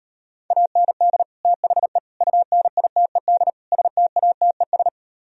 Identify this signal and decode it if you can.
Morse code, heard as a keyed tone, about 38 wpm, 700 Hz. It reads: AND THE UNITED STATES